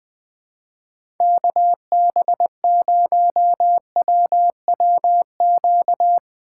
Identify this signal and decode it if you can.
Morse code, heard as a keyed tone, about 21 wpm, 700 Hz. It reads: KB0WWQ
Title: Speed Racing - USA Call Signs 20wpm